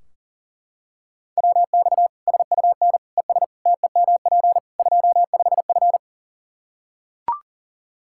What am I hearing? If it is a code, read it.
Morse code, heard as a keyed tone, about 40 wpm, 700 Hz. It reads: WX SUN ES TEMP 25F
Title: Practice QSO - Exchange 2 - Info - 40wpm